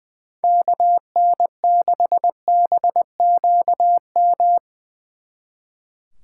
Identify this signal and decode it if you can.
Morse code, heard as a keyed tone, about 22 wpm, 700 Hz. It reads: KN6BQM
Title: Speed Racing - USA 2x3 Call Signs 20wpm